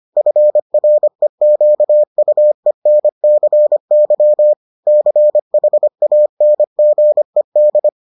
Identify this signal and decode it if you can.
Morse code, heard as a keyed tone, about 25 wpm, 600 Hz. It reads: FREQUENCY CHANGED